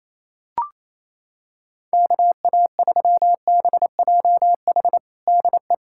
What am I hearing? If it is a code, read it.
Morse code, heard as a keyed tone, about 28 wpm, 700 Hz. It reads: KA3BJH DE
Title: USA Callsign DE Callsign - 6 Characters - 28wpm